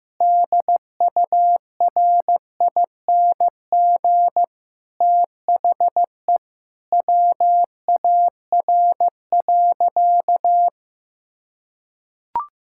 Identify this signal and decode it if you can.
Morse code, heard as a keyed tone, about 15 wpm, 700 Hz. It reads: DURING THE WAR.
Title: Sentences from Top 1000 Words - Review Part 6 - 15wpm